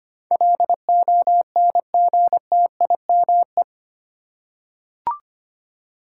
Morse code, heard as a keyed tone, about 25 wpm, 700 Hz. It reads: LONGTIME
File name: 8 Letter Words - First 3 Letters Given - 25wpm